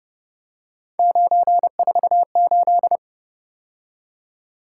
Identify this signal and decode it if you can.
Morse code, heard as a keyed tone, about 29 wpm, 700 Hz. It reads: 948
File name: Speed-Racing - Three Letter-Number Combinations - 25wpm